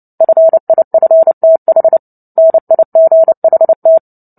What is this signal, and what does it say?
Morse code, heard as a keyed tone, about 30 wpm, 650 Hz. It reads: FIFTH NIGHT
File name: Book Two / 03-Chapter 3 - 29 WPM